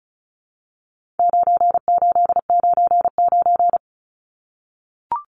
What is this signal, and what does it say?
Morse code, heard as a keyed tone, about 34 wpm, 700 Hz. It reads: 9899
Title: Numbers - 4-Digit 35wpm